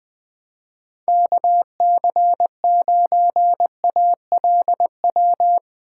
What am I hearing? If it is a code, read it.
Morse code, heard as a keyed tone, about 21 wpm, 700 Hz. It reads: KC9ALW
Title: Speed Racing - USA Call Signs 17wpm